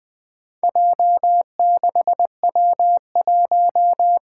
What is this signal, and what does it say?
Morse code, heard as a keyed tone, about 20 wpm, 700 Hz. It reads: J6W1
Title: Four Challenging Letter-Number Combinations 20wpm